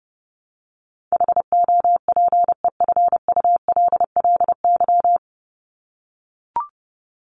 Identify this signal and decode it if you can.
Morse code, heard as a keyed tone, about 30 wpm, 700 Hz. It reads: HOPEFULLY
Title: Top 1251-1500 Words 30wpm